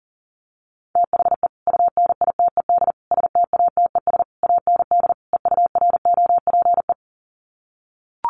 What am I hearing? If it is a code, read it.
Morse code, heard as a keyed tone, about 40 wpm, 700 Hz. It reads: THE UNITED STATES AND EUROPE
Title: Sets of 5 Words Encore 40wpm